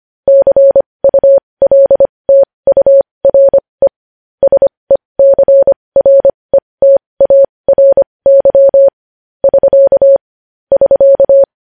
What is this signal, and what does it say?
Morse code, heard as a keyed tone, about 25 wpm, spaced as for 20 wpm, 550 Hz. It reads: CULTURE SECRETARY <SK> <SK>
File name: podcast / News headlines-1x-25x20